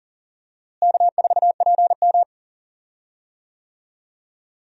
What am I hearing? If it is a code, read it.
Morse code, heard as a keyed tone, about 40 wpm, 700 Hz. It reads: K4PM
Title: USA Call Signs - 4 Characters - Repeated 3x - 40wpm